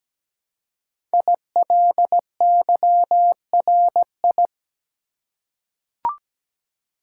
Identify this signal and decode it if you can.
Morse code, heard as a keyed tone, about 17 wpm, 700 Hz. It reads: ILYRI
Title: Code Groups - 5 Letters - 17wpm